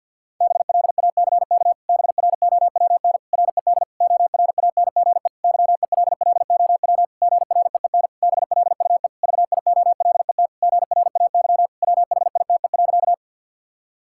Morse code, heard as a keyed tone, about 51 wpm, 700 Hz. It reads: BLACK BROWN RED ORANGE YELLOW GREEN BLUE VIOLET GRAY WHITE.
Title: Pangram Challenge v2 - 50wpm